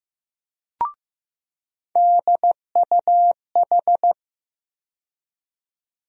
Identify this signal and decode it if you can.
Morse code, heard as a keyed tone, about 15 wpm, 700 Hz. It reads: DUH
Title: USA Call Sign Postfixes - 3 Letters 15wpm